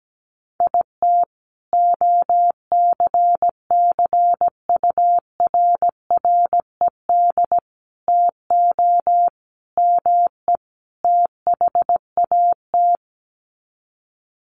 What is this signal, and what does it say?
Morse code, heard as a keyed tone, about 17 wpm, 700 Hz. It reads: IT OCCURRED TO ME THAT
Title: Sets of 5 Words Encore 17wpm